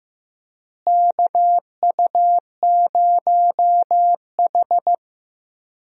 Morse code, heard as a keyed tone, about 15 wpm, 700 Hz. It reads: KU0H
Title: USA Call Signs - 4 Characters - Repeated 3x - 15wpm